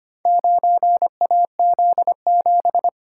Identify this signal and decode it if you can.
Morse code, heard as a keyed tone, about 25 wpm, 700 Hz. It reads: 9AZ7
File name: practice / 25wpm / Four Letter-Number Combinations - 25wpm